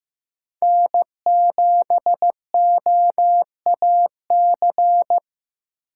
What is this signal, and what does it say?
Morse code, heard as a keyed tone, about 15 wpm, 700 Hz. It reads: N7OAC
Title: USA Call Signs - 6x Character Spacing - 15wpm